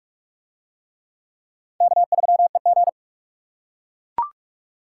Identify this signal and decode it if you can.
Morse code, heard as a keyed tone, about 44 wpm, 700 Hz. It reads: K3EG